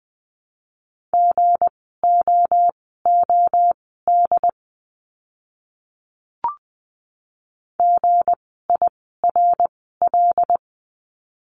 Morse code, heard as a keyed tone, about 20 wpm, spaced as for 14 wpm, 700 Hz. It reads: GOOD GIRL
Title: 160 Lesson 26f - Letter g - Words 20wpm